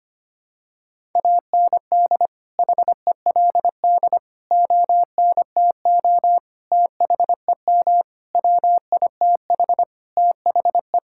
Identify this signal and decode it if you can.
Morse code, heard as a keyed tone, about 25 wpm, 700 Hz. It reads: AND HELD ONTO THEM WITH THE
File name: Sets of 9 Words - 25wpm